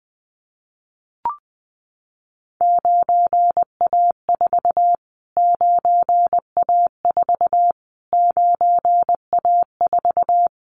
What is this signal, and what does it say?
Morse code, heard as a keyed tone, about 19 wpm, 700 Hz. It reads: 9A4 9A4 9A4
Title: Callsign Prefixes for Top 25 DXCC - Repeated 3x - 20wpm